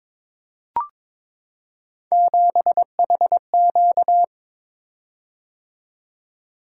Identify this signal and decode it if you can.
Morse code, heard as a keyed tone, about 21 wpm, 700 Hz. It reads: 7HQ
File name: practice / 22wpm / Three Letter-Number Combinations - 22wpm